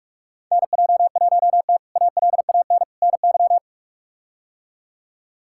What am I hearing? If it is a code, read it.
Morse code, heard as a keyed tone, about 46 wpm, 700 Hz. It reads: NJ1T ALAN NY
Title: K1USN SST QSO Practice - 45wpm